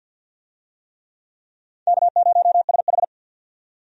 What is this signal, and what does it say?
Morse code, heard as a keyed tone, about 52 wpm, 700 Hz. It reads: K0SH
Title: USA Call Signs - 4 Characters - 50wpm